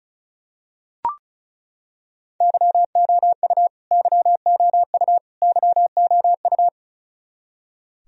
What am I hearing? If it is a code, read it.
Morse code, heard as a keyed tone, about 34 wpm, 700 Hz. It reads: YOU YOU YOU
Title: Top 100 Words Repeated 35wpm